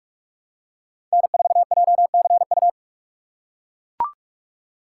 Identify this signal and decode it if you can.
Morse code, heard as a keyed tone, about 42 wpm, 700 Hz. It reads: N4JCU
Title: USA Call Signs - 5 Characters - 3x Character Spacing - 45wpm